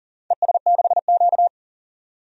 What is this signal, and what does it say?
Morse code, heard as a keyed tone, about 40 wpm, 700 Hz. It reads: ES6Q
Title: DXCC-100 Call Signs - 4 Characters - 40wpm